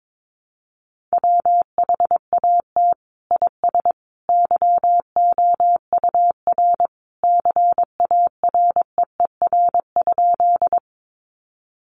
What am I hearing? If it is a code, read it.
Morse code, heard as a keyed tone, about 22 wpm, 700 Hz. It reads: WHAT IS YOUR CAREER?
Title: Sentences from Top 1000 Words 22wpm